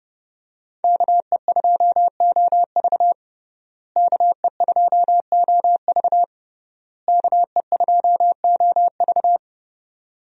Morse code, heard as a keyed tone, about 30 wpm, 700 Hz. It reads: KE2OV KE2OV KE2OV
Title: USA Call Signs - 5 Characters - Repeated 3x - 30wpm